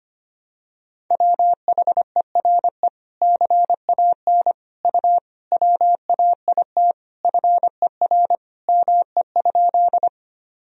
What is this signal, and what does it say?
Morse code, heard as a keyed tone, about 25 wpm, 700 Hz. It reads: WHERE CAN U WAIT FER ME?
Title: Sentences from Top 500 Words and Abbreviations 25wpm